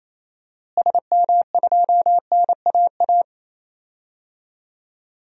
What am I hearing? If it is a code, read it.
Morse code, heard as a keyed tone, about 28 wpm, 700 Hz. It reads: SM2NAA
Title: DXCC-100 Call Signs - 6 Characters - Repeated 3x - 28wpm